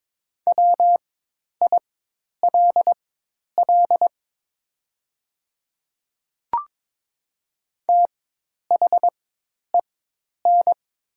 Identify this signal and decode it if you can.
Morse code, heard as a keyed tone, about 22 wpm, spaced as for 10 wpm, 700 Hz. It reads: WILL THEN
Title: 4x Character Spacing - Top 200 Words - 22wpm